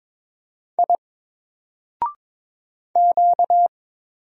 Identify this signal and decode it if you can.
Morse code, heard as a keyed tone, about 22 wpm, 700 Hz. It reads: I Q
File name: Single-Letter-Number-22wpm